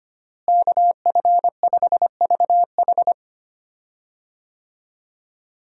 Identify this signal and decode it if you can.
Morse code, heard as a keyed tone, about 25 wpm, 700 Hz. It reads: KF5VH
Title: USA Call Signs - 5 Characters - Repeated 3x - 25wpm